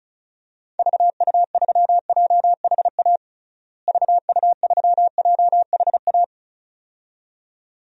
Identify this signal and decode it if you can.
Morse code, heard as a keyed tone, about 36 wpm, 700 Hz. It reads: VU3JHA VU3JHA
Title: DXCC-100 Call Signs - 6 Characters - Repeated 2x - 35wpm